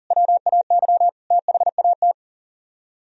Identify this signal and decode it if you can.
Morse code, heard as a keyed tone, about 50 wpm, 700 Hz. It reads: WAY THAT